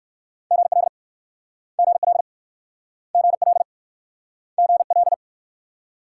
Morse code, heard as a keyed tone, about 54 wpm, 690 Hz. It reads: GL GL GL GL
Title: Speed-Racing Multi-letter Phonemes - 45wpm